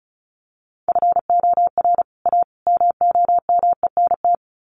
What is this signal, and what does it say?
Morse code, heard as a keyed tone, about 34 wpm, 700 Hz. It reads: FOR A MOMENT
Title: Sets of 3 Words 35wpm